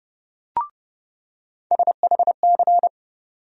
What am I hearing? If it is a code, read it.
Morse code, heard as a keyed tone, about 30 wpm, 700 Hz. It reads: SHC
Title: USA Call Sign Postfixes - 3 Letters 30wpm